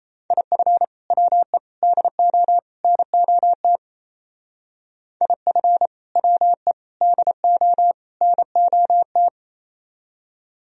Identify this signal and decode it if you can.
Morse code, heard as a keyed tone, about 34 wpm, 700 Hz. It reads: IF WE DO NOT IF WE DO NOT
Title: Speed-Racing - Sets of 4 Words 28wpm